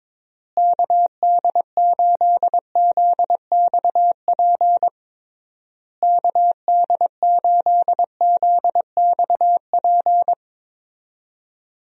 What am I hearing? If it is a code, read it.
Morse code, heard as a keyed tone, about 22 wpm, 700 Hz. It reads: KD8ZXP KD8ZXP
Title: USA Call Signs - 6 Characters - Repeated 3x - 22wpm